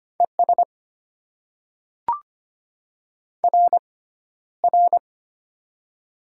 Morse code, heard as a keyed tone, about 25 wpm, spaced as for 13 wpm, 690 Hz. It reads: ES RR
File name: Top 90 QSO Elements - 6x Character Spacing - 25wpm